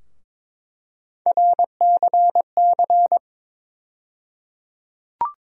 Morse code, heard as a keyed tone, about 22 wpm, 700 Hz. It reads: RCC
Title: USA Call Sign Postfixes - 3 Letters 22wpm